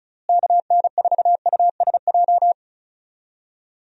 Morse code, heard as a keyed tone, about 34 wpm, 700 Hz. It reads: KN4USJ
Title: USA Call Signs - 6 Characters - Repeated 3x - 35wpm